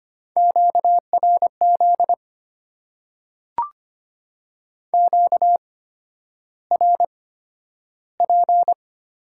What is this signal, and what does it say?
Morse code, heard as a keyed tone, about 25 wpm, spaced as for 11 wpm, 700 Hz. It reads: QRZ QRP
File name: Top 90 QSO Elements - 8x Character Spacing - 25wpm